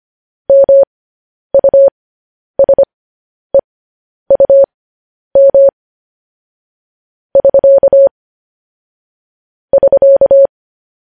Morse code, heard as a keyed tone, about 25 wpm, spaced as for 10 wpm, 550 Hz. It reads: MUSEUM <SK> <SK>